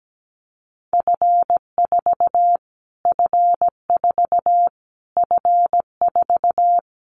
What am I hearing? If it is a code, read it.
Morse code, heard as a keyed tone, about 17 wpm, 700 Hz. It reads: F4 F4 F4
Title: Callsign Prefixes for Top 25 DXCC - Repeated 3x - 17wpm